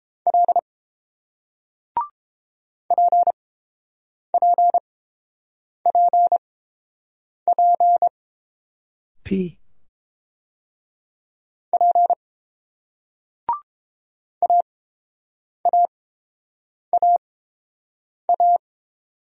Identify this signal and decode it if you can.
Morse code, heard as a keyed tone, about 31 wpm, spaced as for 7 wpm, 690 Hz. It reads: L PPPP P AAAA